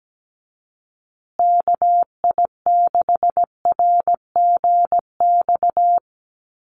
Morse code, heard as a keyed tone, about 17 wpm, 700 Hz. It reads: KI6RGX
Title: USA Call Signs Repeated 2x 17wpm